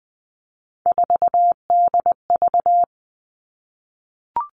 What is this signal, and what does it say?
Morse code, heard as a keyed tone, about 20 wpm, 700 Hz. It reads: 4DV
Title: Three Challenging Letter-Number Combinations 20wpm